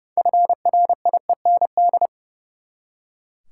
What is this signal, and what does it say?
Morse code, heard as a keyed tone, about 29 wpm, 700 Hz. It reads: FRIEND
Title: Six Letter Words - 30wpm